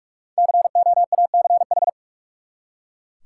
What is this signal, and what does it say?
Morse code, heard as a keyed tone, about 41 wpm, 700 Hz. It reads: COACH
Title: Five Letter Words - 45wpm